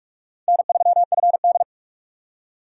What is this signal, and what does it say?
Morse code, heard as a keyed tone, about 57 wpm, 700 Hz. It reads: N3FD